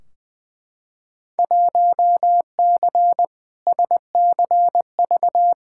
Morse code, heard as a keyed tone, about 20 wpm, 700 Hz. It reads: 1C SCV
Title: Field Day Exchanges - 20wpm